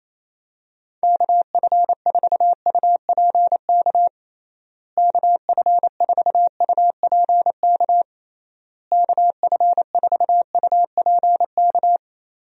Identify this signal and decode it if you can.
Morse code, heard as a keyed tone, about 28 wpm, 700 Hz. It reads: KF4UPK KF4UPK KF4UPK
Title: USA Call Signs - 6 Characters - Repeated 3x - 28wpm